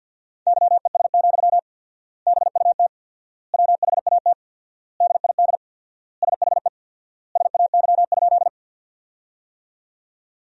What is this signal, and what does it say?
Morse code, heard as a keyed tone, about 50 wpm, 700 Hz. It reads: YES, BUT WHAT DID SHE SAY?
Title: 4x Word-Spacing - Sentences from Top 500 Words - 50wpm